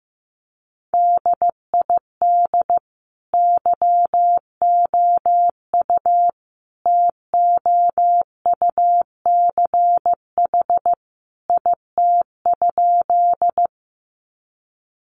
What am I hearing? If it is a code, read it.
Morse code, heard as a keyed tone, about 15 wpm, 700 Hz. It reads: DID YOU TOUCH IT?
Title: Sentences from Top 1200 Words 15wpm